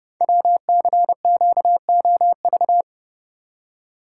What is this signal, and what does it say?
Morse code, heard as a keyed tone, about 30 wpm, 700 Hz. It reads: WCQOV
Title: Code Groups - 5 Letters - 30wpm